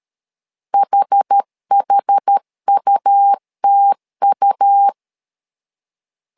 Morse code, heard as a keyed tone, about 13 wpm, 800 Hz. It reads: HHUTU
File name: LEZIONI / 2 UHDN / 2 UHD 12wpm 1540